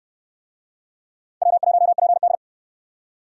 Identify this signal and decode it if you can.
Morse code, heard as a keyed tone, about 66 wpm, 700 Hz. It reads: W1PD